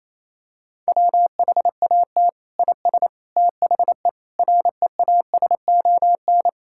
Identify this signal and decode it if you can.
Morse code, heard as a keyed tone, about 28 wpm, 700 Hz. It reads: WHAT IS THE REASON
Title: Sentences from Top 1000 Words - Review Part 5 - 28wpm